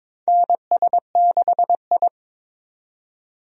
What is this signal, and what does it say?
Morse code, heard as a keyed tone, about 22 wpm, 700 Hz. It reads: NS6I
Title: USA Call Signs - 4 Characters - 22wpm